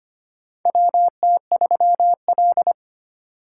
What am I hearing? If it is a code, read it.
Morse code, heard as a keyed tone, about 25 wpm, 700 Hz. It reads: WT3L
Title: USA Call Signs - 4x Character Spacing - 25wpm